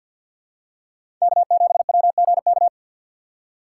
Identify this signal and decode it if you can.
Morse code, heard as a keyed tone, about 48 wpm, 700 Hz. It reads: K7WGK
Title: USA Call Signs - 5 Characters - 4x Character Spacing - 50wpm